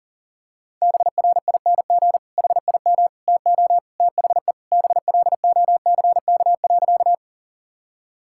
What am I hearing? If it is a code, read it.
Morse code, heard as a keyed tone, about 37 wpm, 700 Hz. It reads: BRING HIM TO THE BLOCK.
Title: Sentences from Top 1600 Words 40wpm